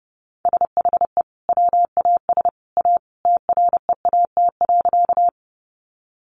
Morse code, heard as a keyed tone, about 30 wpm, 700 Hz. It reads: SHE WAS A TREAT.
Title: Sentences from Top 1300 Words 30wpm